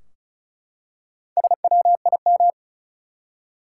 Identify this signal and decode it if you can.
Morse code, heard as a keyed tone, about 36 wpm, 700 Hz. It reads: SWIM
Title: Four Letter Words - 35wpm